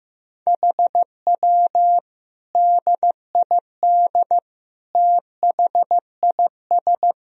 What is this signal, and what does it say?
Morse code, heard as a keyed tone, about 15 wpm, 700 Hz. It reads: HW DID THIS
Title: Sentences from Top 400 Words and Abbreviations 15wpm